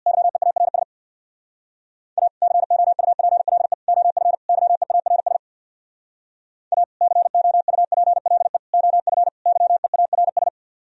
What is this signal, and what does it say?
Morse code, heard as a keyed tone, about 64 wpm, 700 Hz. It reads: YEARS A COUPLE OF YEARS A COUPLE OF YEARS